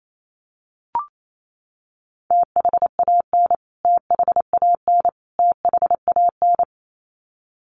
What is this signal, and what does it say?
Morse code, heard as a keyed tone, about 28 wpm, 700 Hz. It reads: THAN THAN THAN
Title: Top 300 Words Repeated 28wpm